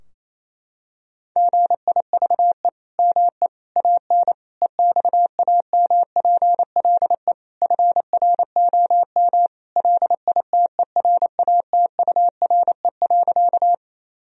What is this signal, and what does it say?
Morse code, heard as a keyed tone, about 28 wpm, 700 Hz. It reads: GIVE ME AN EXAMPLE FROM LITERATURE.
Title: Sentences from Top 1000 Words - Review Part 3 - 28wpm